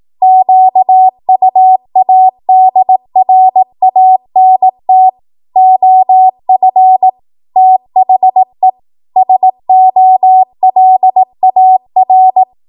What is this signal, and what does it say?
Morse code, heard as a keyed tone, about 18 wpm, 750 Hz. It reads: QUADRANT OF THE SOLAR
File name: Archive / 18 WPM / 18WPM